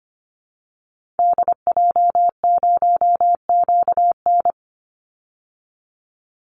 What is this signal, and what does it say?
Morse code, heard as a keyed tone, about 25 wpm, 700 Hz. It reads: DJ0QN
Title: Call Signs Repeated 3x - 25wpm